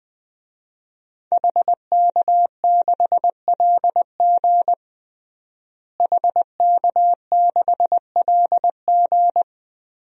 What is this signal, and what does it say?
Morse code, heard as a keyed tone, about 20 wpm, 700 Hz. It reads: HK6LG HK6LG
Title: DXCC-100 Call Signs - 5 Characters - Repeated 3x - 20wpm